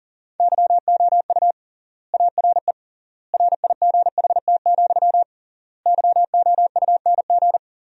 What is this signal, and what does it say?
Morse code, heard as a keyed tone, about 40 wpm, 700 Hz. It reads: YOU ARE RIGHT, YOUNG
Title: 3x Word-Spacing - Sentences from Top 400 Words - 40wpm